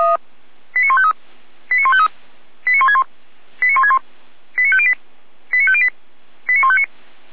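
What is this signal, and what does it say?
Five-tone selective call (ZVEI), heard as quick runs of five tones, tones of about 70 ms. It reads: 89153 89153 89161 89161 89508 89508 89159
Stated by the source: SDR-received ZVEI five-tone selective call logged by audio.f85360.feuerwehr-orlamuende.org, fire brigade Orlamünde.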